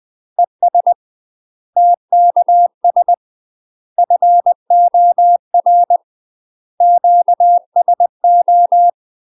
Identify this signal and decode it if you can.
Morse code, heard as a keyed tone, about 20 wpm, 700 Hz. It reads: ES TKS FOR QSO
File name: UK MASTER QSOs TRAINING / 20WPM / UK MASTER QSOs 0034